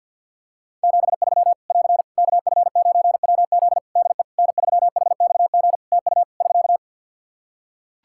Morse code, heard as a keyed tone, about 46 wpm, 700 Hz. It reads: 73 <AR> KF9WZ DE N3HXG TU <SK>
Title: Practice QSO - Exchange 4 - Ending - 50wpm